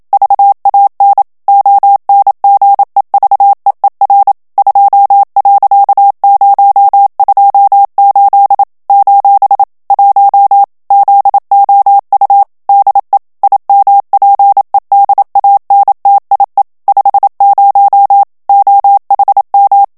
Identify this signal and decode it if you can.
Morse code, heard as a keyed tone, about 27 wpm, 800 Hz. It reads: VAN ONGEVEER 2.028 : 1 ZOU DE IMPEDANTIE 50 OHM
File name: HOLLANDAIS texte clair 1 vitesse 26 mots minute